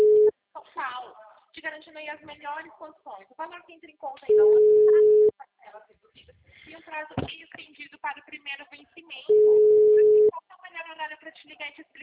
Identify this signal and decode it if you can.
Ringback tone (Europe, 425 Hz), with speech.